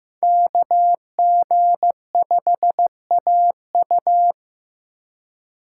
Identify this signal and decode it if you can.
Morse code, heard as a keyed tone, about 15 wpm, 700 Hz. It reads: KG5AU
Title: USA Call Signs - 5 Characters - 2x Character Spacing - 15wpm